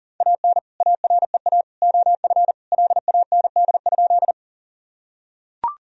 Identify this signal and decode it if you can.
Morse code, heard as a keyed tone, about 40 wpm, 700 Hz. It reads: AN AREA OF LAND?
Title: Sentences from Top 1000 Words - Review Part 2 - 40wpm